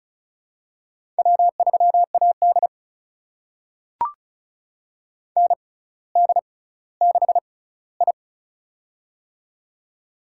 Morse code, heard as a keyed tone, about 36 wpm, spaced as for 18 wpm, 700 Hz. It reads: W3AD ND6I
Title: USA Call Signs - 4 Characters - 6x Character Spacing - 35wpm